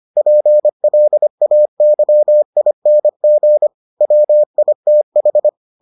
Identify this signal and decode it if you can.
Morse code, heard as a keyed tone, about 25 wpm, 600 Hz. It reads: PLAYING WITH